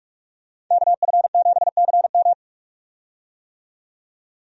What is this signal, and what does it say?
Morse code, heard as a keyed tone, about 46 wpm, 700 Hz. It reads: KF7CM
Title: USA Call Signs - 5 Characters - 45wpm